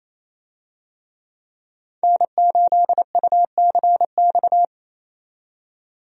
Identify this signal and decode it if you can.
Morse code, heard as a keyed tone, about 28 wpm, 700 Hz. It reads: N8UCX
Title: USA Call Signs - 5 Characters - 28wpm